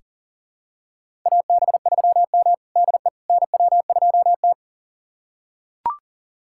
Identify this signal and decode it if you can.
Morse code, heard as a keyed tone, about 40 wpm, 690 Hz. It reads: AB3M DE NW2T
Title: USA Callsign DE Callsign - 4 Characters - 40wpm